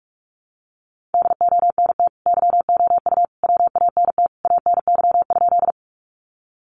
Morse code, heard as a keyed tone, about 44 wpm, 700 Hz. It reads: DONT YOU WANT ANY?
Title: Sentences from Top 300 Words 45wpm